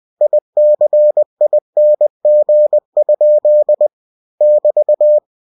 Morse code, heard as a keyed tone, about 20 wpm, 600 Hz. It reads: ICING? <BT>
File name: cw-books / winnie-the-pooh / winnie-20wpm-page-0102